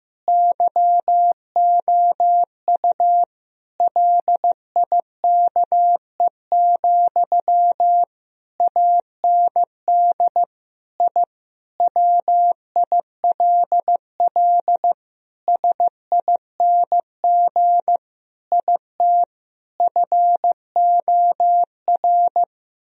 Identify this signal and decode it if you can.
Morse code, heard as a keyed tone, about 16 wpm, 700 Hz. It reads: YOU LIKE, AND I WILL SING IT FOR
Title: Sentences from Top 1000 Words - Review Part 2 - 15wpm